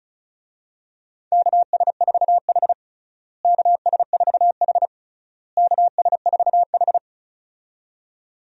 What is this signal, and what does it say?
Morse code, heard as a keyed tone, about 36 wpm, 700 Hz. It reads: KS4H KS4H KS4H
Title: USA Call Signs - 4 Characters - Repeated 3x - 35wpm